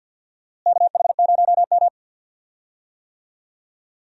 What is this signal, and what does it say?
Morse code, heard as a keyed tone, about 52 wpm, 700 Hz. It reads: KH0M
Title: USA Call Signs - 4 Characters - Repeated 2x - 50wpm